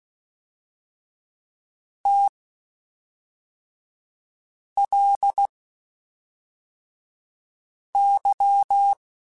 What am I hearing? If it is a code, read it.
Morse code, heard as a keyed tone, about 16 wpm, spaced as for 3 wpm, 800 Hz. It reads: TLY
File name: Spaced Characters / Spaced Characters-mp3s / Weather5 22 sp.@5